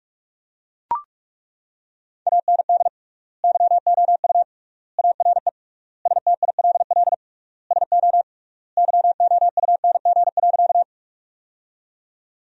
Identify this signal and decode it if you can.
Morse code, heard as a keyed tone, about 41 wpm, 700 Hz. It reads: AND YOU ARE STILL SO YOUNG.
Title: 3x Word-Spacing - Sentences from Top 400 Words - 45wpm